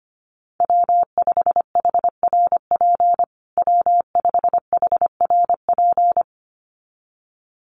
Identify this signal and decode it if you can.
Morse code, heard as a keyed tone, about 25 wpm, 700 Hz. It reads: W5HRP W5HRP
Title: USA Call Signs Repeated 2x 25wpm